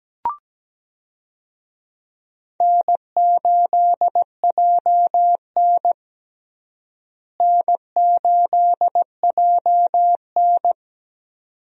Morse code, heard as a keyed tone, about 17 wpm, 700 Hz. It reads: N8JN N8JN
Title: USA Call Signs - 4 Characters - Repeated 2x - 17wpm